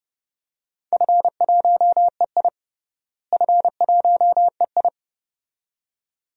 Morse code, heard as a keyed tone, about 32 wpm, 700 Hz. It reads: F1EI F1EI
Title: DXCC-100 Call Signs - 4 Characters - Repeated 2x - 30wpm